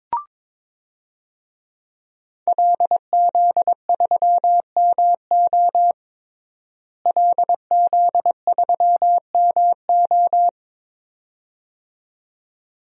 Morse code, heard as a keyed tone, about 22 wpm, 700 Hz. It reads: LZ3MO LZ3MO
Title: DXCC-100 Call Signs - 5 Characters - Repeated 2x - 22wpm